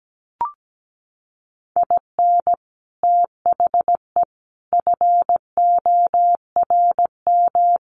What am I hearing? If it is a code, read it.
Morse code, heard as a keyed tone, about 17 wpm, 700 Hz. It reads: IN THE FORM
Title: Sets of 5 Words Encore2 17wpm